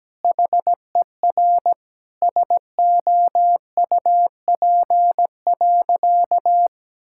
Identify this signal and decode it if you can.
Morse code, heard as a keyed tone, about 17 wpm, 700 Hz. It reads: HER SOUP.
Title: Sentences from Top 1000 Words - Review Part 6 - 17wpm